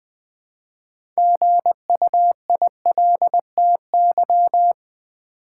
Morse code, heard as a keyed tone, about 20 wpm, 700 Hz. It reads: GUILTY
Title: Build A Word - 2 Letters At A Time - 6-Letter Words - 20wpm